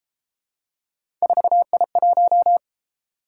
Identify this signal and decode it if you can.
Morse code, heard as a keyed tone, about 33 wpm, 700 Hz. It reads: 4I1